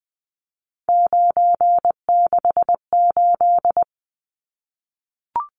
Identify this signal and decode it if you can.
Morse code, heard as a keyed tone, about 20 wpm, 700 Hz. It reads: 968
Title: Numbers - 3-Digit 20wpm